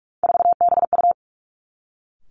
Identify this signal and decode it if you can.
Morse code, heard as a keyed tone, about 46 wpm, 700 Hz. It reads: 4BU
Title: Three Challenging Letter-Number Combinations 45wpm